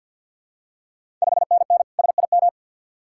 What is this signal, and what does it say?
Morse code, heard as a keyed tone, about 45 wpm, 700 Hz. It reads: 5NN SIM
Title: Ontario QSO Party Exchange 50wpm